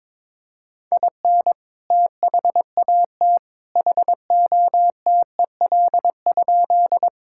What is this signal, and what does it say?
Morse code, heard as a keyed tone, about 22 wpm, 700 Hz. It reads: IN THAT HOTEL?
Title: Sentences from Top 1600 Words 22wpm